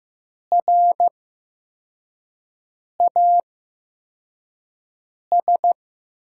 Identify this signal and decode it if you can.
Morse code, heard as a keyed tone, about 15 wpm, spaced as for 4 wpm, 700 Hz. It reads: RAS
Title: USA Call Signs - 5 Characters - 8x Character Spacing - 15wpm